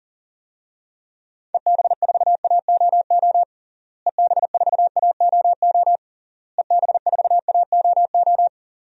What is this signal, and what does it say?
Morse code, heard as a keyed tone, about 37 wpm, 700 Hz. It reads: EB4AOO EB4AOO EB4AOO
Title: DXCC-100 Call Signs - 6 Characters - Repeated 3x - 40wpm